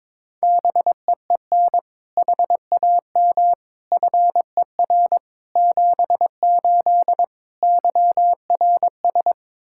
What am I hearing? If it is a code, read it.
Morse code, heard as a keyed tone, about 22 wpm, 700 Hz. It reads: BEEN HAM FER 78 YRS
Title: Practice QSO - Exchange 3 - More Info - 22wpm